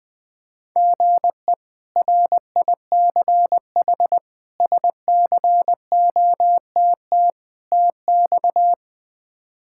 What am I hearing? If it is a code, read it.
Morse code, heard as a keyed tone, about 20 wpm, 700 Hz. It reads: GE RICH SCOTT TX
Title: K1USN SST QSO Practice - 20wpm